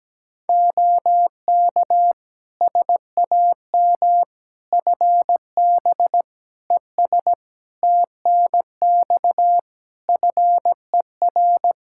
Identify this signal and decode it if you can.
Morse code, heard as a keyed tone, about 17 wpm, 700 Hz. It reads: OK SAM FB ES TNX FER
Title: Practice QSO - Exchange 2 - Info - 17wpm